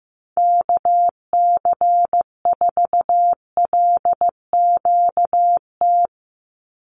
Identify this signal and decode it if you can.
Morse code, heard as a keyed tone, about 15 wpm, 700 Hz. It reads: KC4LQT
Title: USA Call Signs 15wpm